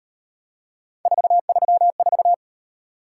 Morse code, heard as a keyed tone, about 38 wpm, 700 Hz. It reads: 434